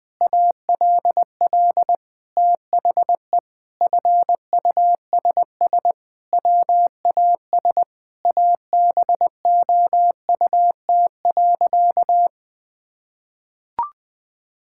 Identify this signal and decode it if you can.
Morse code, heard as a keyed tone, about 20 wpm, 700 Hz. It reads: ALL THE FUSS WAS ABOUT.
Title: Sentences with 8-Word Phrases - 20wpm